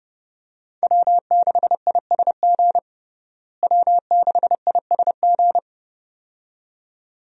Morse code, heard as a keyed tone, about 32 wpm, 700 Hz. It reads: W6ISG W6ISG
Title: USA Call Signs - 5 Characters - Repeated 2x - 30wpm